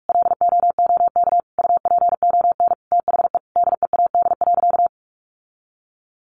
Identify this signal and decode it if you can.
Morse code, heard as a keyed tone, about 44 wpm, 700 Hz. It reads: LOOK UPON THE DEAD.
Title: Sentences from Top 1500 Words 45wpm